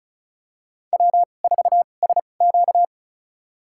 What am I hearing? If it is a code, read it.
Morse code, heard as a keyed tone, about 36 wpm, spaced as for 25 wpm, 700 Hz. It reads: W4SQ
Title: USA Call Signs - 4 Characters - 2x Character Spacing - 35wpm